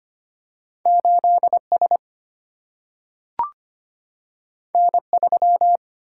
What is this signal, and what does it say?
Morse code, heard as a keyed tone, about 25 wpm, 700 Hz. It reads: 8 S N 3
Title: Single-Letter Single-Number 25wpm